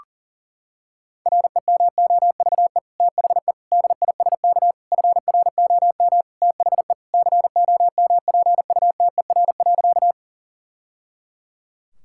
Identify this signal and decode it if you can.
Morse code, heard as a keyed tone, about 40 wpm, 700 Hz. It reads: REMOVE THE DISK FROM THE COMPUTER.
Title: Sentences from Top 1000 Words - Review Part 6 - 40wpm